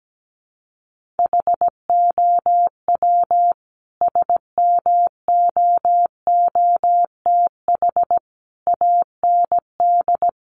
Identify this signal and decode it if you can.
Morse code, heard as a keyed tone, about 17 wpm, 700 Hz. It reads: HOW SMOOTH AND
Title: Sentences from Top 1400 Words 17wpm